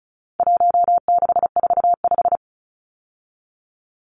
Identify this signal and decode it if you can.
Morse code, heard as a keyed tone, about 34 wpm, 700 Hz. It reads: 1645
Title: Numbers - 4-Digit 35wpm